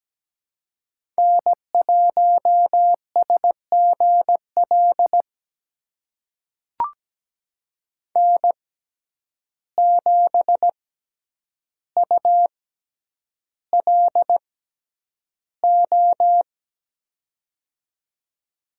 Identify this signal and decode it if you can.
Morse code, heard as a keyed tone, about 17 wpm, spaced as for 9 wpm, 700 Hz. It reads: N1SGL N7ULO
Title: USA Call Signs - 5 Characters - 6x Character Spacing - 17wpm